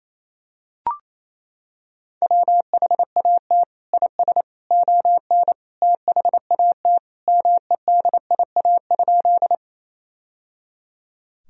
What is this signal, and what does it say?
Morse code, heard as a keyed tone, about 28 wpm, 700 Hz. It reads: WHAT IS ON THAT MEDIA?
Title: Sentences from Top 1000 Words - Review Part 4 - 28wpm